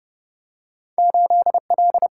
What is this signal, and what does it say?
Morse code, heard as a keyed tone, about 30 wpm, 700 Hz. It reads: 8L
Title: Single-Letter Single-Number 30wpm